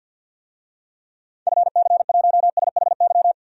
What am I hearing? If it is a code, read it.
Morse code, heard as a keyed tone, about 52 wpm, 700 Hz. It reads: FC1SHY
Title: DXCC-100 Call Signs - 6 Characters - 50wpm